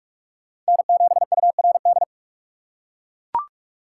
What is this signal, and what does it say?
Morse code, heard as a keyed tone, about 42 wpm, 700 Hz. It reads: N7URD